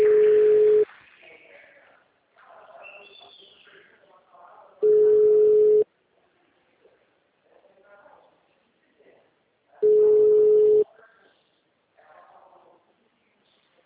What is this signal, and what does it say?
Ringback tone (Europe, 425 Hz), with speech.
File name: out- -2004- - - .2749